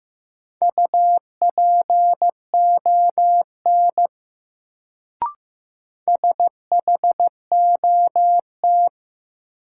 Morse code, heard as a keyed tone, about 15 wpm, 700 Hz. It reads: UPON SHOT
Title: Sentences from Top 1100 Words - New Word Drill Alt - 15wpm